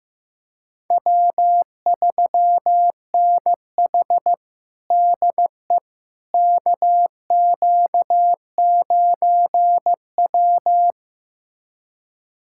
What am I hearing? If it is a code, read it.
Morse code, heard as a keyed tone, about 15 wpm, 700 Hz. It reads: W3NH DE KQ9W
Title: USA Callsign DE Callsign - 4 Characters - 15wpm